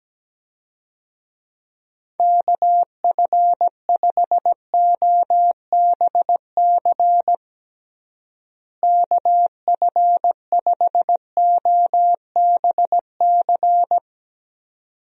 Morse code, heard as a keyed tone, about 17 wpm, 700 Hz. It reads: KF5OBC KF5OBC
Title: USA Call Signs - 6 Characters - Repeated 2x - 17wpm